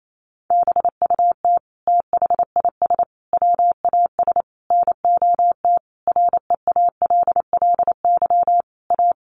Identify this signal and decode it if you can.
Morse code, heard as a keyed tone, about 28 wpm, 700 Hz. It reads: BUT THIS WAS NOT REALLY A
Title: Sentences from Top 1500 Words 28wpm